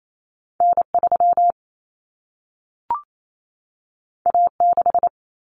Morse code, heard as a keyed tone, about 28 wpm, 700 Hz. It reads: N3 A6
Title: Common Cut Numbers - 2-Digit 28wpm